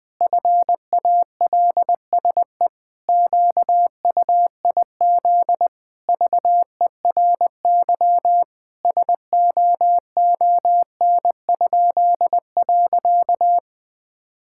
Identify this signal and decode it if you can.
Morse code, heard as a keyed tone, about 21 wpm, 700 Hz. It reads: FALSE QUIZ VERY SOON?.
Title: Pangram Challenge v2 - 20wpm